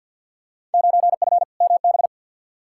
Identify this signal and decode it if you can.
Morse code, heard as a keyed tone, about 52 wpm, 700 Hz. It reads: 9F MB
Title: Field Day Exchanges Encore - 50wpm